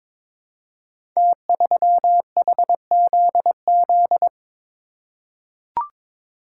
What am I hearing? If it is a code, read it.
Morse code, heard as a keyed tone, about 22 wpm, 700 Hz. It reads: T3HZZ
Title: Code Groups - 5 Letter:Numbers - 22wpm